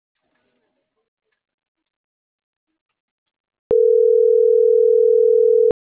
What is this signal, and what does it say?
Ringback tone (North America), 440 + 480 Hz.